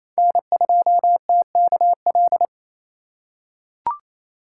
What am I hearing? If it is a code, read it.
Morse code, heard as a keyed tone, about 27 wpm, 700 Hz. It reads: N2TKL
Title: USA Call Signs - 4x Character Spacing - 28wpm